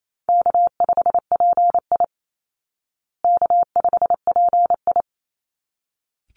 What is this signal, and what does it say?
Morse code, heard as a keyed tone, about 28 wpm, 700 Hz. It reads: K5PI K5PI
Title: Easy Call Signs Repeated 2x - 28wpm